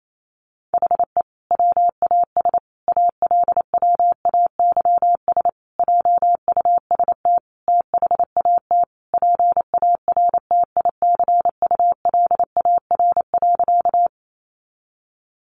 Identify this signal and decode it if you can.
Morse code, heard as a keyed tone, about 28 wpm, 700 Hz. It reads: HE WAS ALWAYS JUST THAT PARTICULAR.
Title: Sentences from Top 600 Words 28wpm